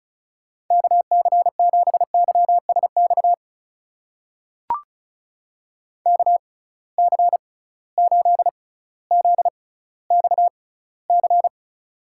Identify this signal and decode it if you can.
Morse code, heard as a keyed tone, about 36 wpm, spaced as for 18 wpm, 700 Hz. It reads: KC7YSX KC8ZXC
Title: USA Call Signs - 6 Characters - 6x Character Spacing - 35wpm